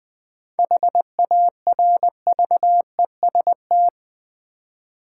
Morse code, heard as a keyed tone, about 20 wpm, 700 Hz. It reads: HARVEST
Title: 7 Letter Words - First 3 Letters Given - 20wpm